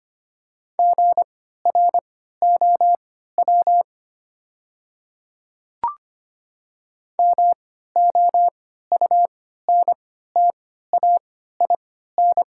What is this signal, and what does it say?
Morse code, heard as a keyed tone, about 25 wpm, spaced as for 14 wpm, 700 Hz. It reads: GROW MOUNTAIN
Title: 3x Character Spacing - Top 300 Words - 25wpm